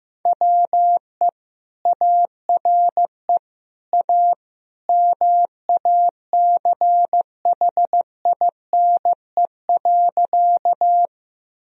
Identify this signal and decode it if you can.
Morse code, heard as a keyed tone, about 15 wpm, 700 Hz. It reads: WE ARE A MACHINE.
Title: Sentences from Top 1000 Words - Review Part 2 - 15wpm